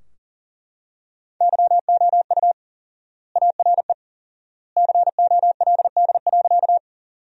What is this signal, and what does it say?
Morse code, heard as a keyed tone, about 40 wpm, 700 Hz. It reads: YOU ARE COLD.
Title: 4x Word-Spacing - Sentences from Top 400 Words - 40wpm